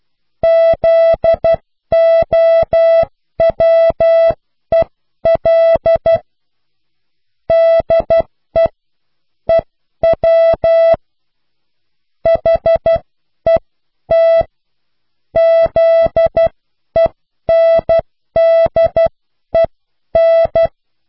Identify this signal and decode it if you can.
Morse code, heard as a keyed tone, about 12 wpm, spaced as for 10 wpm, 660 Hz. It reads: ZOWEL DE EW HET ZENDEN